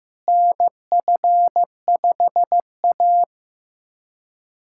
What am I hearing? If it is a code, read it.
Morse code, heard as a keyed tone, about 15 wpm, 700 Hz. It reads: NF5A
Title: USA Callsign DE Callsign - 4 Characters - 15wpm